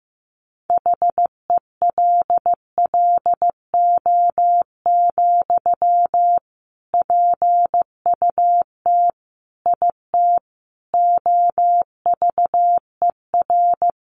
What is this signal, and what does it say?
Morse code, heard as a keyed tone, about 15 wpm, 700 Hz. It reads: HELLO, PUT IT OVER
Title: Sentences from Top 1100 Words 15wpm